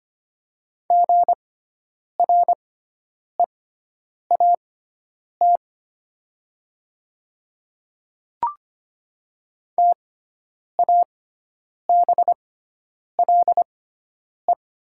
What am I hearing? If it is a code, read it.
Morse code, heard as a keyed tone, about 25 wpm, spaced as for 9 wpm, 700 Hz. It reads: GREAT TABLE
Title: 6x Character Spacing - Top 500 Words - 25wpm